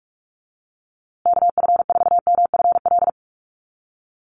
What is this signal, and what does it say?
Morse code, heard as a keyed tone, about 44 wpm, 700 Hz. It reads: KF4MFL
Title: USA Call Signs Repeated 2x 45wpm